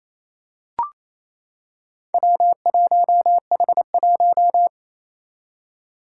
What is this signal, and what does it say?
Morse code, heard as a keyed tone, about 28 wpm, 700 Hz. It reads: W1H1
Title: Four Challenging Letter-Number Combinations 28wpm